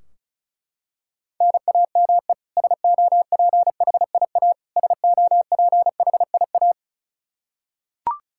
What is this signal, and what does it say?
Morse code, heard as a keyed tone, about 36 wpm, 700 Hz. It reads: NAME SOPHIA SOPHIA
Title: Practice QSO - Exchange 1 - Beginning - 35wpm